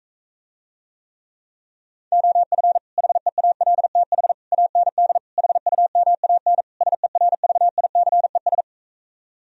Morse code, heard as a keyed tone, about 45 wpm, 700 Hz. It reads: OF HEALTH AND HUMAN SERVICES